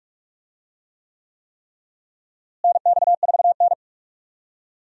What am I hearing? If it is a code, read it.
Morse code, heard as a keyed tone, about 45 wpm, 700 Hz. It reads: NX4N
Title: USA Call Signs - 4 Characters - Repeated 3x - 45wpm